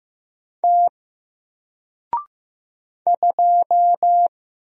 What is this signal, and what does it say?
Morse code, heard as a keyed tone, about 15 wpm, 700 Hz. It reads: T 2
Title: Single Letter-Number Alt 15wpm